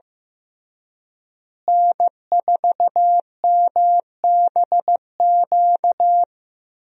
Morse code, heard as a keyed tone, about 15 wpm, 700 Hz. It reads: N4MBQ
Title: USA Call Signs - 5 Characters - Repeated 3x - 15wpm